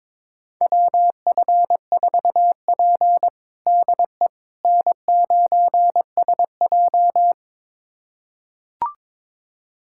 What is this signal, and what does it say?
Morse code, heard as a keyed tone, about 21 wpm, 700 Hz. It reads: WF4P DE N9SJ
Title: USA Callsign DE Callsign - 4 Characters - 22wpm